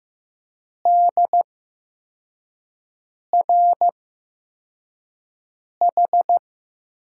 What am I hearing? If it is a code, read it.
Morse code, heard as a keyed tone, about 15 wpm, spaced as for 4 wpm, 700 Hz. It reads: DRH
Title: USA Call Signs - 6 Characters - 8x Character Spacing - 15wpm